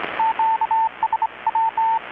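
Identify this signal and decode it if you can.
Morse code, heard as a keyed tone, about 25 wpm, 910 Hz, with radio noise. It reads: QSW
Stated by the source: Maritime radiotelegraphy coast station event transmission, received in Chiapas, Mexico about 3,800 km away on a Kenwood R-600 receiver with a 10-metre grounded longwire antenna.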